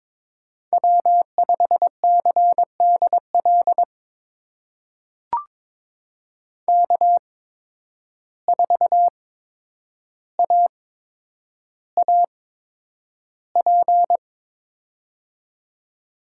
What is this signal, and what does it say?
Morse code, heard as a keyed tone, about 22 wpm, spaced as for 9 wpm, 700 Hz. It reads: W5CDL K4AAP
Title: USA Call Signs - 5 Characters - 8x Character Spacing - 22wpm